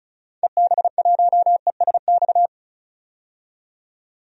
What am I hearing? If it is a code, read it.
Morse code, heard as a keyed tone, about 34 wpm, 700 Hz. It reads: EB1ESX
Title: DXCC-100 Call Signs - 6 Characters - 35wpm